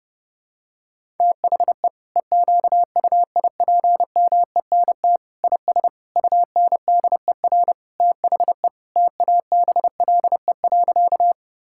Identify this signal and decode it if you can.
Morse code, heard as a keyed tone, about 30 wpm, 700 Hz. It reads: THE EQUIPMENT IS UNDER THE TABLE.
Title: Sentences from Top 1000 Words - Review Part 5 - 30wpm